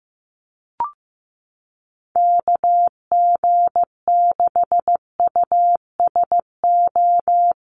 Morse code, heard as a keyed tone, about 15 wpm, 700 Hz. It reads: KG6USO
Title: USA Call Signs Repeated 3x 15wpm